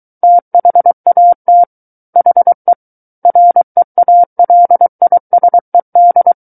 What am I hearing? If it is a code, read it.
Morse code, heard as a keyed tone, about 23 wpm, 700 Hz. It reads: THAT HE REALISED
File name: Book One / 14-Chapter 14 - 23 WPM